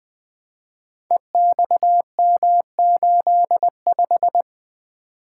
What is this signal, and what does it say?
Morse code, heard as a keyed tone, about 20 wpm, 700 Hz. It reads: EXM85
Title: Code Groups - 5 Letter:Numbers - 20wpm